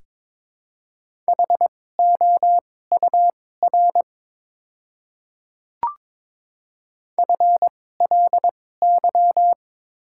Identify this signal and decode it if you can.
Morse code, heard as a keyed tone, about 22 wpm, spaced as for 16 wpm, 690 Hz. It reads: HOUR FLY
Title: 2x Character Spacing - Top 500 Words - 22wpm